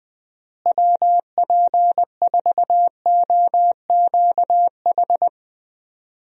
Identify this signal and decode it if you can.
Morse code, heard as a keyed tone, about 20 wpm, 700 Hz. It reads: WP4OQH
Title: USA Call Signs - 6 Characters - Repeated 2x - 20wpm